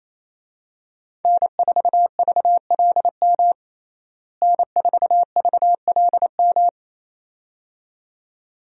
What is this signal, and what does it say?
Morse code, heard as a keyed tone, about 28 wpm, 700 Hz. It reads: N4VLM N4VLM
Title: USA Call Signs - 5 Characters - Repeated 2x - 28wpm